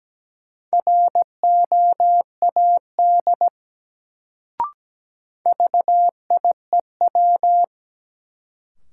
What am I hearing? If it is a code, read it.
Morse code, heard as a keyed tone, about 17 wpm, 700 Hz. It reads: ROAD VIEW
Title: Sets of 5 Words - New Word Drill - 17wpm